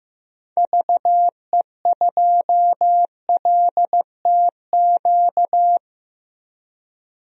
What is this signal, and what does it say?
Morse code, heard as a keyed tone, about 15 wpm, 700 Hz. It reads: VE2LTQ
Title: DXCC-100 Call Signs - 6 Characters - 15wpm